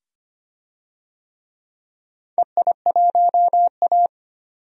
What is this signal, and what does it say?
Morse code, heard as a keyed tone, about 25 wpm, 700 Hz. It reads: EI1A
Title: DXCC-100 Call Signs - 4 Characters - 25wpm